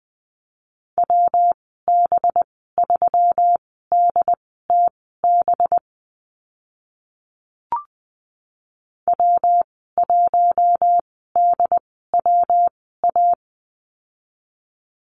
Morse code, heard as a keyed tone, about 20 wpm, spaced as for 14 wpm, 700 Hz. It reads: WB3DTB W1DWA
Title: 193 Lesson 31g - Letter v - Call Signs 20wpm